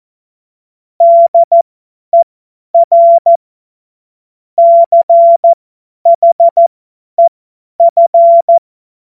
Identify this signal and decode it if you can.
Morse code, heard as a keyed tone, about 14 wpm, spaced as for 10 wpm, 680 Hz. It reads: DER CHEF
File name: Morsekurs / Witze / Witz 16. Mai 2023 50 F70